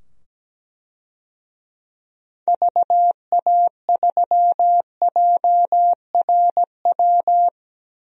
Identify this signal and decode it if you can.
Morse code, heard as a keyed tone, about 17 wpm, 700 Hz. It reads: VA3JRW
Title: DXCC-100 Call Signs - 6 Characters - 17wpm